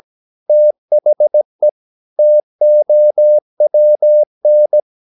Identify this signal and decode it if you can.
Morse code, heard as a keyed tone, about 17 wpm, 600 Hz. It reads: THE TOWN
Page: Romanen Never af Ken Follet på engelsk.